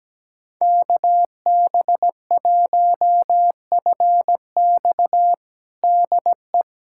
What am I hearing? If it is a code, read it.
Morse code, heard as a keyed tone, about 17 wpm, 700 Hz. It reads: KB1FX DE
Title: USA Callsign DE Callsign - 5 Characters - 17wpm